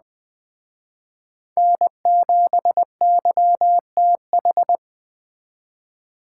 Morse code, heard as a keyed tone, about 21 wpm, 700 Hz. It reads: N7YTH